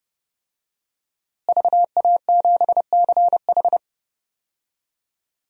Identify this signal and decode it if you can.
Morse code, heard as a keyed tone, about 30 wpm, 700 Hz. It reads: VA7CH
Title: DXCC-100 Call Signs - 5 Characters - Repeated 2x - 30wpm